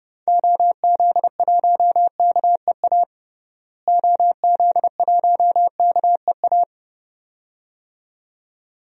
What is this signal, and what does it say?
Morse code, heard as a keyed tone, about 30 wpm, 700 Hz. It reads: OZ1KEA OZ1KEA
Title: DXCC-100 Call Signs - 6 Characters - Repeated 2x - 30wpm